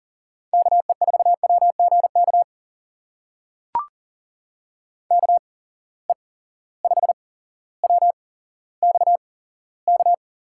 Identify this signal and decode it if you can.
Morse code, heard as a keyed tone, about 40 wpm, spaced as for 17 wpm, 700 Hz. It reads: KE4WGK KE5WXK
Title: USA Call Signs - 6 Characters - 8x Character Spacing - 40wpm